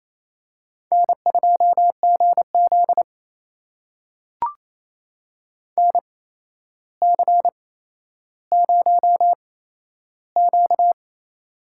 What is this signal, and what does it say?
Morse code, heard as a keyed tone, about 28 wpm, spaced as for 12 wpm, 700 Hz. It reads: N2GZ NC0Q
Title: USA Call Signs - 4 Characters - 8x Character Spacing - 28wpm